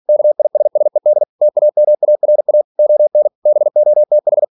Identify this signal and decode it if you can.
Morse code, heard as a keyed tone, about 46 wpm, 600 Hz. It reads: KISSED TAMARA ON BOTH